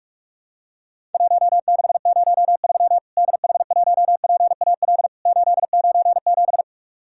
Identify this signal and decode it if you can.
Morse code, heard as a keyed tone, about 41 wpm, 700 Hz. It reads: 1603 DH1PAL 897
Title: Worked All Europe DX Contest - QTC Exchange - 45wpm